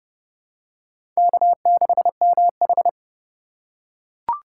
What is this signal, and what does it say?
Morse code, heard as a keyed tone, about 30 wpm, 700 Hz. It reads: K6MH
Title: USA Call Signs - 4 Characters - 3x Character Spacing - 30wpm